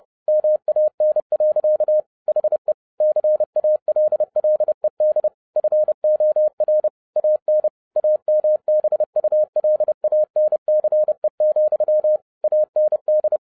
Morse code, heard as a keyed tone, about 30 wpm, 610 Hz. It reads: MAN. HE CALLED FOR AN AMBULANCE, AND
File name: 2011-07-mp3 / n55 BACK FROM A BIG ACCIDENT 30wpm